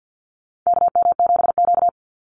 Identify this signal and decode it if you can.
Morse code, heard as a keyed tone, about 50 wpm, 700 Hz. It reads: KM7Q
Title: Call Signs Repeated 2x - 50wpm